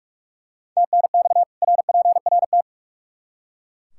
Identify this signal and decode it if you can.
Morse code, heard as a keyed tone, about 43 wpm, 700 Hz. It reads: TNX RPRT
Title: Practice QSO - Exchange 2 - Info - 45wpm